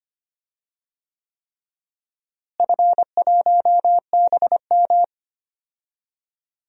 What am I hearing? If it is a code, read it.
Morse code, heard as a keyed tone, about 25 wpm, 700 Hz. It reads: F1BM
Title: DXCC-100 Call Signs - 4 Characters - Repeated 3x - 25wpm